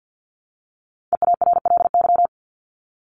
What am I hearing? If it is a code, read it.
Morse code, heard as a keyed tone, about 50 wpm, 700 Hz. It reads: EARLY
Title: Top 500 Words Repeated 50wpm